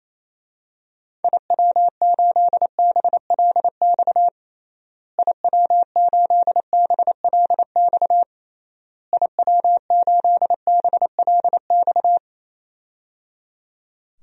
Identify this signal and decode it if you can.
Morse code, heard as a keyed tone, about 28 wpm, 700 Hz. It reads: IW8BLX IW8BLX IW8BLX
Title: DXCC-100 Call Signs - 6 Characters - Repeated 3x - 28wpm